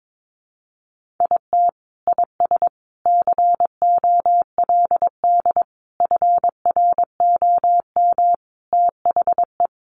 Morse code, heard as a keyed tone, about 22 wpm, 700 Hz. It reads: IT IS COLD FROM THE
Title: Sentences from Top 1400 Words 22wpm